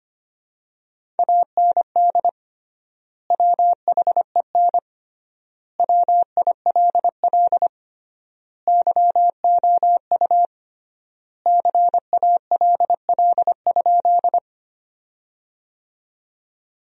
Morse code, heard as a keyed tone, about 25 wpm, 700 Hz. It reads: AND WHEN WILL YOU CALL?
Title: 3x Word-Spacing - Sentences from Top 100 Words - 25wpm